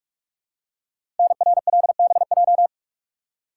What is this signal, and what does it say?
Morse code, heard as a keyed tone, about 44 wpm, 700 Hz. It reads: NRLBJ
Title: Code Groups - 5 Letters - 45wpm